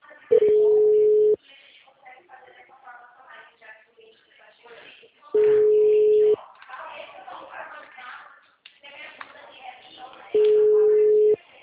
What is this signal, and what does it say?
Ringback tone (Europe, 425 Hz), with speech.
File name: out- -2008- - - .35271